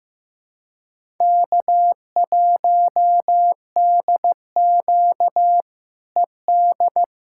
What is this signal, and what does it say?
Morse code, heard as a keyed tone, about 15 wpm, 700 Hz. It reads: K1DQ ED
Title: K1USN SST QSO Practice - 15wpm